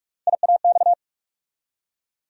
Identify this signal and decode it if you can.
Morse code, heard as a keyed tone, about 47 wpm, 700 Hz. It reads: IAX
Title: USA Call Sign Postfixes - 3 Letters 45wpm